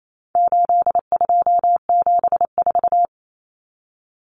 Morse code, heard as a keyed tone, about 28 wpm, 700 Hz. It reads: 8274
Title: Numbers - 4-Digit 28wpm